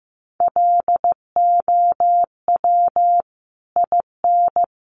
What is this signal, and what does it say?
Morse code, heard as a keyed tone, about 15 wpm, 700 Hz. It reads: LOW IN
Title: Sentences from Top 900 Words 15wpm